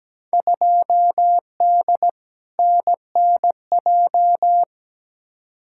Morse code, heard as a keyed tone, about 17 wpm, 700 Hz. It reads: 2D NNJ
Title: Field Day Exchanges - 17wpm